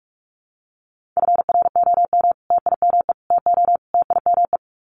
Morse code, heard as a keyed tone, about 45 wpm, 700 Hz. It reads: FROM TIME TO TIME
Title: Sets of 4 Words 45wpm